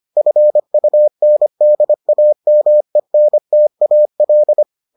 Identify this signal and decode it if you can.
Morse code, heard as a keyed tone, about 25 wpm, 600 Hz. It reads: FUNDAMENTAL